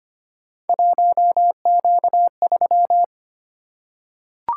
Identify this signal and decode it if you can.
Morse code, heard as a keyed tone, about 25 wpm, 700 Hz. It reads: 1Q3
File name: Three Letter-Number Combinations - 25wpm